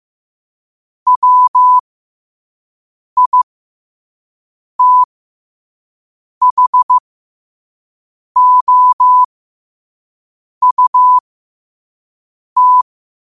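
Morse code, heard as a keyed tone, about 15 wpm, spaced as for 5 wpm, 1000 Hz. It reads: WITHOUT
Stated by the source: 5 wpm Morse code practice recording.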